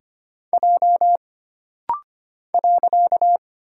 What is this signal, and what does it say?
Morse code, heard as a keyed tone, about 25 wpm, 700 Hz. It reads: J .
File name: Single Character (Letter:Number:Punctuation:Prosign) - Rapid-Fire - 25wpm